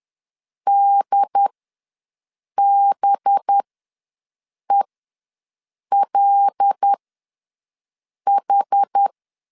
Morse code, heard as a keyed tone, about 10 wpm, spaced as for 6 wpm, 800 Hz. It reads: DBELH
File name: LEZIONI / 6 PKX0 / 6 PKX0 ws 10wpm 4220